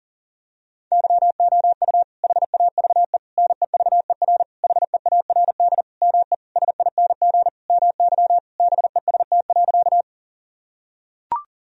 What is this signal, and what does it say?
Morse code, heard as a keyed tone, about 37 wpm, 700 Hz. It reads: YOU HAVE NEVER HEARD ME SING MY BEST.
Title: Sentences from Top 1000 Words - Review Part 2 - 40wpm